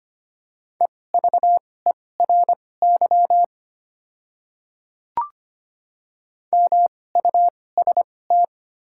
Morse code, heard as a keyed tone, about 25 wpm, spaced as for 18 wpm, 700 Hz. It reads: EVERY MUST
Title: 2x Character Spacing - Top 300 Words - 25wpm